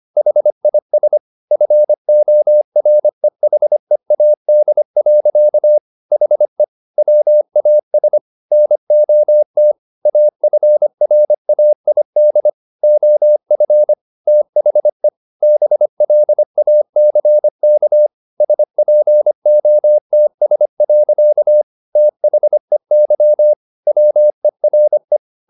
Morse code, heard as a keyed tone, about 25 wpm, 600 Hz. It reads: HIS FOREHEAD. HE WAS NOT AFRAID OF THE BLACK SPOTS. THEY WERE